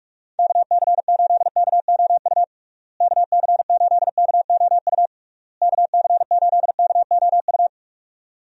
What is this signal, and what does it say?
Morse code, heard as a keyed tone, about 42 wpm, 700 Hz. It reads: KC8KOU KC8KOU KC8KOU
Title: USA Call Signs - 6 Characters - Repeated 3x - 45wpm